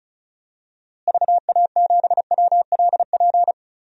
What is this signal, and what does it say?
Morse code, heard as a keyed tone, about 36 wpm, 700 Hz. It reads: VA7WLP
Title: DXCC-100 Call Signs - 6 Characters - Repeated 2x - 35wpm